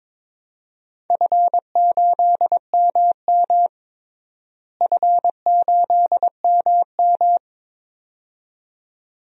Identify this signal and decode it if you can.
Morse code, heard as a keyed tone, about 22 wpm, 700 Hz. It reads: F8MM F8MM
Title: DXCC-100 Call Signs - 4 Characters - Repeated 2x - 22wpm